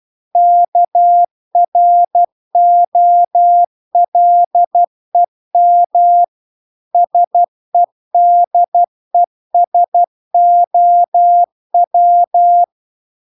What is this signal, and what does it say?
Morse code, heard as a keyed tone, about 12 wpm, 700 Hz. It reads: KROLEM SEDESOW